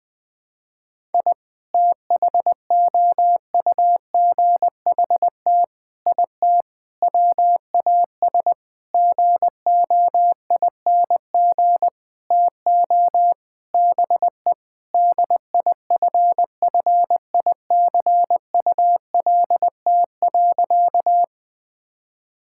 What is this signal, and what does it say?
Morse code, heard as a keyed tone, about 20 wpm, 700 Hz. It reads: I THOUGHT IT WAS GOING TO BE DIFFICULT.
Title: Sentences with 7-Word Phrases - 20wpm